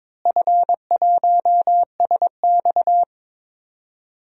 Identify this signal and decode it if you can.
Morse code, heard as a keyed tone, about 22 wpm, 700 Hz. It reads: F1SX
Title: DXCC-100 Call Signs - 4 Characters - Repeated 3x - 22wpm